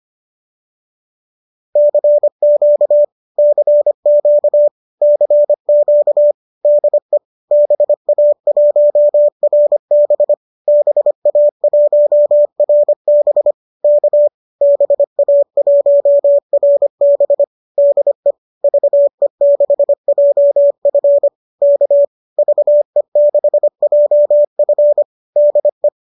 Morse code, heard as a keyed tone, about 25 wpm, 590 Hz. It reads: CQ CQ CQ DE BA1RB BA1RB K BA1RB DE VE6JF K VE6JF DE
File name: qso203 25